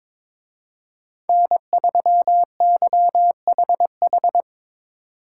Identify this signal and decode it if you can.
Morse code, heard as a keyed tone, about 25 wpm, 700 Hz. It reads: N3YHH
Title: Speed Racing - USA 1x3 Call Signs 22wpm